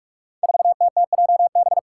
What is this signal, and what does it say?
Morse code, heard as a keyed tone, about 41 wpm, 700 Hz. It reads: 4TTJB
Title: Five Challenging Letter-Number Combinations 45wpm